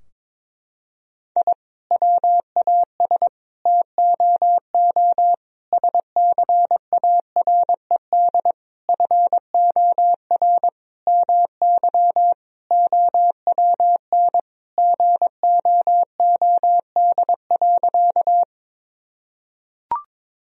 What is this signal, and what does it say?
Morse code, heard as a keyed tone, about 21 wpm, 700 Hz. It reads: I WAS TOO SCARED FOR MY OWN GOOD.
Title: Sentences from Top 1000 Words - Review Part 4 - 22wpm